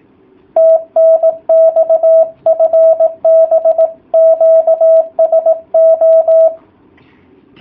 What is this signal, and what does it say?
Morse code, heard as a keyed tone, about 18 wpm, 660 Hz. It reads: TNX FB QSO